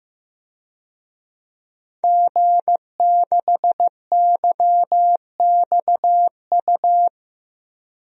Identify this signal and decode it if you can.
Morse code, heard as a keyed tone, about 15 wpm, 700 Hz. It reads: G6YXU
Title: DXCC-100 Call Signs - 5 Characters - 15wpm